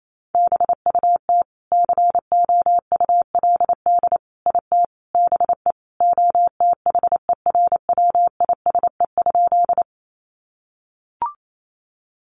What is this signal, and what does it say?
Morse code, heard as a keyed tone, about 28 wpm, 700 Hz. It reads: BUT COULD IT BE OTHERWISE?
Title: Sentences from Top 1200 Words 28wpm